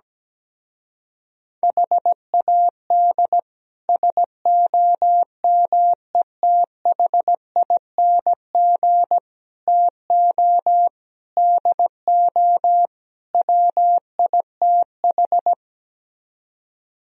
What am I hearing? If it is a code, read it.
Morse code, heard as a keyed tone, about 17 wpm, 700 Hz. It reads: HAD SOMETHING TO DO WITH
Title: Speed-Racing - Sets of 5 Words 17wpm